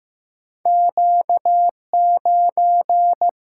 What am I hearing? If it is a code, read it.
Morse code, heard as a keyed tone, about 15 wpm, 700 Hz. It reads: Q 9
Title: Single-Letter Single-Number 15wpm